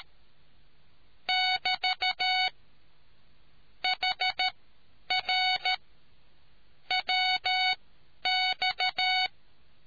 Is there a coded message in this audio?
Yes: Morse code from 1.3 to 9.3 s, heard as a keyed tone, about 13 wpm, spaced as for 9 wpm, 740 Hz. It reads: <BT> HR WX